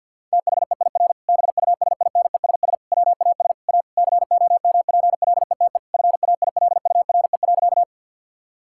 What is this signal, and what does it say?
Morse code, heard as a keyed tone, about 50 wpm, 700 Hz. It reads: THEIR BUSINESS WAS A COMPLETE FAILURE.
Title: Sentences from Top 1000 Words - Review Part 6 - 50wpm